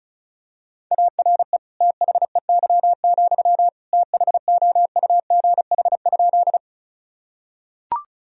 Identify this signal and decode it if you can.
Morse code, heard as a keyed tone, about 36 wpm, 700 Hz. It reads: ARE THEY, THOUGH?
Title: Sentences from Top 1000 Words - Review Part 2 - 35wpm